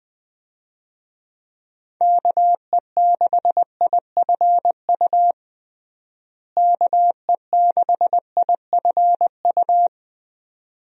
Morse code, heard as a keyed tone, about 21 wpm, 700 Hz. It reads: KE6IFU KE6IFU
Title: USA Call Signs - 6 Characters - Repeated 2x - 20wpm